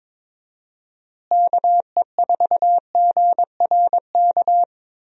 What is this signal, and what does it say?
Morse code, heard as a keyed tone, about 22 wpm, 700 Hz. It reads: KE4GRK
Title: USA Call Signs - 6 Characters - Repeated 2x - 22wpm